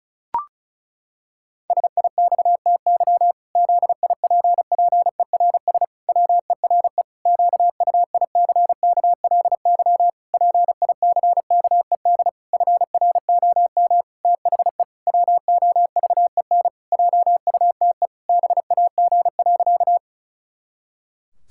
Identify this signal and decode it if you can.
Morse code, heard as a keyed tone, about 36 wpm, 700 Hz. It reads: SIXTY ZIPPERS WERE QUICKLY PICKED FROM THE WOVEN JUTE BAG.
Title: Pangram Challenge v2 - 35wpm